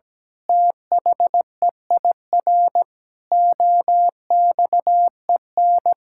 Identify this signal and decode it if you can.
Morse code, heard as a keyed tone, about 17 wpm, 700 Hz. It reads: THEIR OXEN
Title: Pangram Challenge v2 - 17wpm